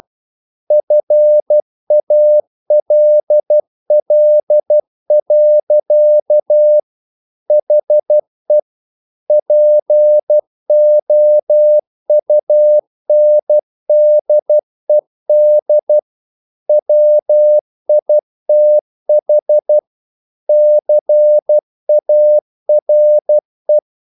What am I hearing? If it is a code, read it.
Morse code, heard as a keyed tone, about 12 wpm, 600 Hz. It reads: FALL. HE POUNDED WITH CARE
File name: mp3 files / the gardener - 12wpm